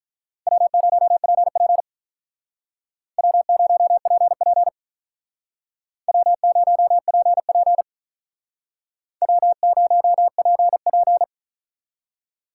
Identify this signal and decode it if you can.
Morse code, heard as a keyed tone, about 46 wpm, 700 Hz. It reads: W0PP W0PP W0PP W0PP